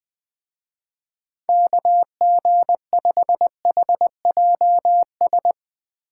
Morse code, heard as a keyed tone, about 22 wpm, 700 Hz. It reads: KG5HJS
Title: Speed Racing - USA 2x3 Call Signs 20wpm